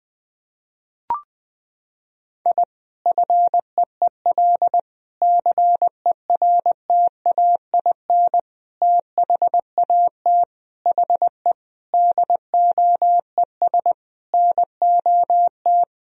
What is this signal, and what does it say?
Morse code, heard as a keyed tone, about 20 wpm, 700 Hz. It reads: I FEEL CERTAIN THAT HE DOES NOT
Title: Sentences from Top 1000 Words - Review Part 2 - 20wpm